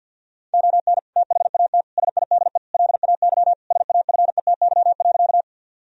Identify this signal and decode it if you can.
Morse code, heard as a keyed tone, about 50 wpm, 700 Hz. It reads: ON THAT SIDE LAY SAFETY.
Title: Sentences from Top 1000 Words - Review Part 4 - 50wpm